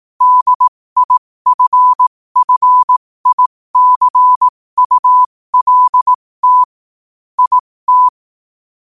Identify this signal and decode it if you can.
Morse code, heard as a keyed tone, about 18 wpm, spaced as for 15 wpm, 1000 Hz. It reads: DIFFICULT IT
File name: qotdfiles / 15wpm